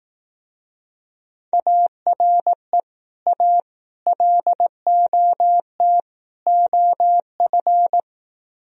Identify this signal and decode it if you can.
Morse code, heard as a keyed tone, about 18 wpm, 700 Hz. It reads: ARE A LOT OF
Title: Speed-Racing - Sets of 4 Words 15wpm